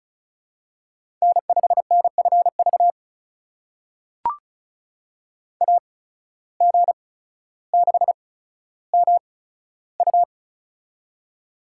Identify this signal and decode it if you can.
Morse code, heard as a keyed tone, about 36 wpm, spaced as for 15 wpm, 700 Hz. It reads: N5NFV AG6MU
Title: USA Call Signs - 5 Characters - 8x Character Spacing - 35wpm